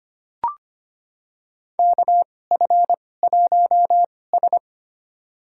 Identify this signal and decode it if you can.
Morse code, heard as a keyed tone, about 25 wpm, spaced as for 18 wpm, 700 Hz. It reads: KF1S
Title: USA Call Signs - 4 Characters - 2x Character Spacing - 25wpm